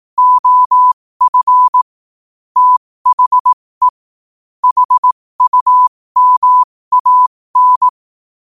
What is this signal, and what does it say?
Morse code, heard as a keyed tone, about 18 wpm, spaced as for 15 wpm, 1000 Hz. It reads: OF THE HUMAN
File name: qotdfiles / 15wpm